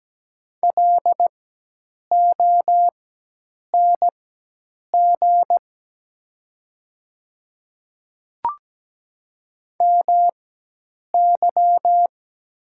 Morse code, heard as a keyed tone, about 17 wpm, spaced as for 8 wpm, 700 Hz. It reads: LONG MY
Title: 4x Character Spacing - Top 200 Words - 17wpm